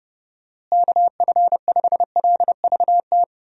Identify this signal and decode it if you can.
Morse code, heard as a keyed tone, about 32 wpm, 700 Hz. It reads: KF5LVT
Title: USA Call Signs - 6 Characters - Repeated 3x - 30wpm